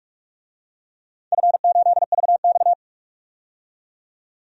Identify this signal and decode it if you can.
Morse code, heard as a keyed tone, about 45 wpm, 700 Hz. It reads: F8VX
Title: DXCC-100 Call Signs - 4 Characters - 45wpm